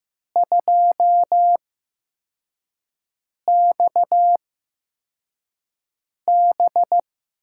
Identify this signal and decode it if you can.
Morse code, heard as a keyed tone, about 15 wpm, spaced as for 4 wpm, 700 Hz. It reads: 2XB
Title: USA Call Signs - 4 Characters - 8x Character Spacing - 15wpm